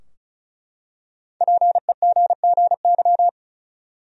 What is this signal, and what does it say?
Morse code, heard as a keyed tone, about 36 wpm, 700 Hz. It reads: PEGGY
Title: USA Common Names - Max 5 Letters - 35wpm